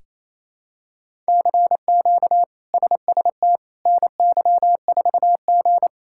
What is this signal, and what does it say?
Morse code, heard as a keyed tone, about 28 wpm, 700 Hz. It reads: CQ SST NY4G
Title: K1USN SST QSO Practice - 28wpm